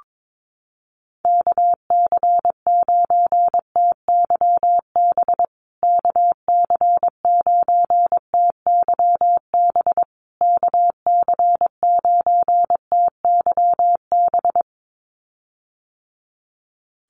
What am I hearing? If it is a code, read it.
Morse code, heard as a keyed tone, about 22 wpm, 700 Hz. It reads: KC9TYB KC9TYB KC9TYB
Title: USA Call Signs Repeated 3x 22wpm